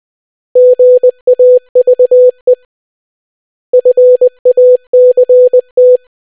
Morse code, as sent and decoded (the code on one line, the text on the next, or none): --. .- ...- . / ..-. .- -.-. -
GAVE FACT